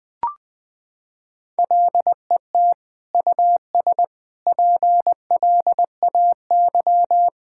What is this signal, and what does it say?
Morse code, heard as a keyed tone, about 20 wpm, 700 Hz. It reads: LET US PLAY
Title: Sentences from Top 1600 Words 20wpm